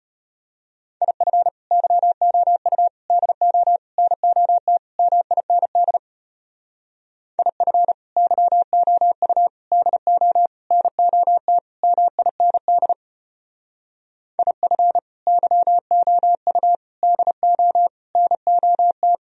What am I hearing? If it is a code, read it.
Morse code, heard as a keyed tone, about 33 wpm, 700 Hz. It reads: IF YOU DO NOT MIND IF YOU DO NOT MIND IF YOU DO NOT